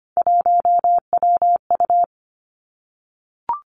Morse code, heard as a keyed tone, about 25 wpm, 700 Hz. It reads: 1WU
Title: Three Challenging Letter-Number Combinations 25wpm